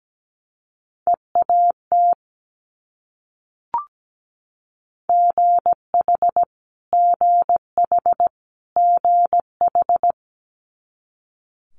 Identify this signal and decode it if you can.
Morse code, heard as a keyed tone, about 17 wpm, 700 Hz. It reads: EAT GH GH GH
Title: Multi-letter Phonemes Repeated 3x 17wpm